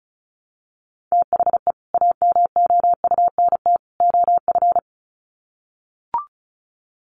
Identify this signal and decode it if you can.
Morse code, heard as a keyed tone, about 34 wpm, 700 Hz. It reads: THE AMOUNT OF
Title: Sets of 3 Words 35wpm